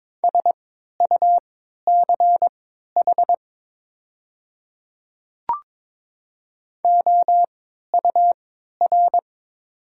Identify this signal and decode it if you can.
Morse code, heard as a keyed tone, about 22 wpm, spaced as for 12 wpm, 690 Hz. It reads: SUCH OUR
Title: 3x Character Spacing - Top 300 Words - 22wpm